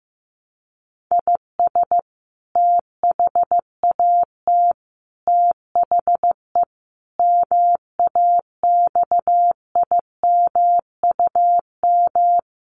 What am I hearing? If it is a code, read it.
Morse code, heard as a keyed tone, about 15 wpm, 700 Hz. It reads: IS THAT THE MAXIMUM
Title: Sentences from Top 1300 Words 15wpm